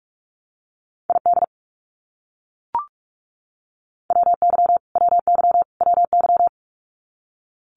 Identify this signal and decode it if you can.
Morse code, heard as a keyed tone, about 44 wpm, 700 Hz. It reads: ID WY WY WY
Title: US States Repeated 3x 45wpm